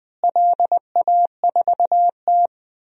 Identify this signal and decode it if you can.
Morse code, heard as a keyed tone, about 20 wpm, 700 Hz. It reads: LA4T
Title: DXCC-100 Call Signs - 4 Characters - Repeated 2x - 20wpm